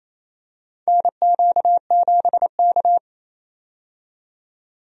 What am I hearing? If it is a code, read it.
Morse code, heard as a keyed tone, about 28 wpm, 700 Hz. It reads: NQ7K
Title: USA Call Signs - 4 Characters - 28wpm